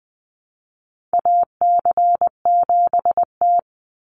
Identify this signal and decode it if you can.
Morse code, heard as a keyed tone, about 20 wpm, 700 Hz. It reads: AC7T
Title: Call Signs - 20wpm